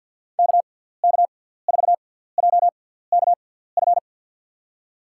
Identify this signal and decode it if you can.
Morse code, heard as a keyed tone, about 52 wpm, spaced as for 17 wpm, 700 Hz. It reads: KK4JKF
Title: USA Call Signs - 6 Characters - 6x Character Spacing - 50wpm